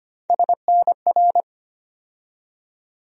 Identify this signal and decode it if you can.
Morse code, heard as a keyed tone, about 25 wpm, 700 Hz. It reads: SNR
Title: USA Call Sign Postfixes - 3 Letters 25wpm